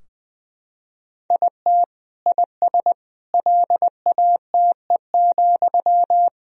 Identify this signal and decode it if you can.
Morse code, heard as a keyed tone, about 20 wpm, 700 Hz. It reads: IT IS LATE,
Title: Sentences from Top 1000 Words - Review Part 3 - 20wpm